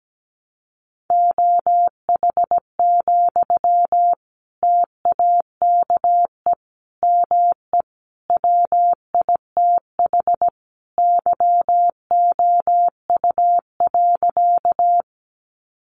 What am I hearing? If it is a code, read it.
Morse code, heard as a keyed tone, about 17 wpm, 700 Hz. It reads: OH, TAKE ME WITH YOU.
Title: Sentences from Top 500 Words 17wpm